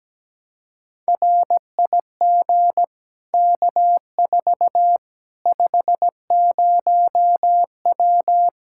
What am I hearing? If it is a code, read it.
Morse code, heard as a keyed tone, about 17 wpm, 700 Hz. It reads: RIG K4 50W
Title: Practice QSO - Exchange 2 - Info - 17wpm